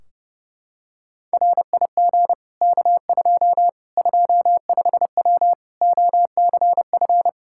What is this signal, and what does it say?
Morse code, heard as a keyed tone, about 30 wpm, 700 Hz. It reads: RIG K2 25W OCF
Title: Practice QSO - Exchange 2 - Info - 30wpm